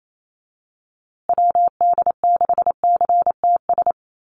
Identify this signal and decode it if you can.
Morse code, heard as a keyed tone, about 28 wpm, 700 Hz. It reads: WD6CTS
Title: USA Call Signs Repeated 3x 28wpm